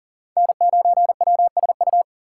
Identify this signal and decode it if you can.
Morse code, heard as a keyed tone, about 40 wpm, 700 Hz. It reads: N9WSU
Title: USA Call Signs - 5 Characters - 4x Character Spacing - 40wpm